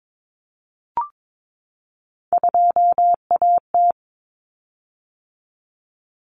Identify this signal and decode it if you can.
Morse code, heard as a keyed tone, about 22 wpm, 700 Hz. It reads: 2AT
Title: Common Cut Numbers - 3-Digit 22wpm